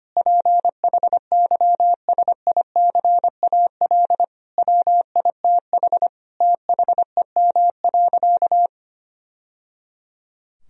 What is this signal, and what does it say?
Morse code, heard as a keyed tone, about 25 wpm, 700 Hz. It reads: PHYSICAL WITH THEM.
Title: Sentences from Top 1000 Words - Review Part 4 - 25wpm